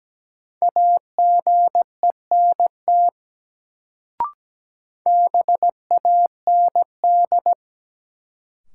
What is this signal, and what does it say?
Morse code, heard as a keyed tone, about 17 wpm, 700 Hz. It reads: AGENT BAND
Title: Sentences from Top 1600 Words - New Word Drill Alt - 17wpm